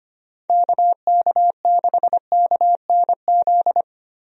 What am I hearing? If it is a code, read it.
Morse code, heard as a keyed tone, about 25 wpm, 700 Hz. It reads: KK6KNZ
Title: USA Call Signs - 6 Characters - 25wpm